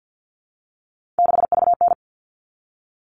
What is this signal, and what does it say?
Morse code, heard as a keyed tone, about 50 wpm, 700 Hz. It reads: 6VN
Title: All Cut Numbers - 3-Digit 50wpm